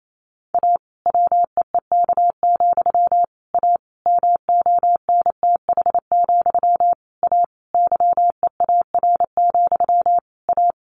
Morse code, heard as a keyed tone, about 28 wpm, 700 Hz. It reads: A WEEK, A MONTH, A YEAR, A
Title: A Sound Of Thunder 28wpm